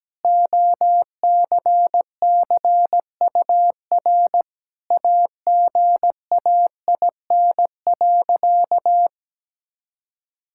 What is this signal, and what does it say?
Morse code, heard as a keyed tone, about 17 wpm, 700 Hz. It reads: OCCUR AGAIN.
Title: Sentences from Top 1000 Words - Review Part 6 - 17wpm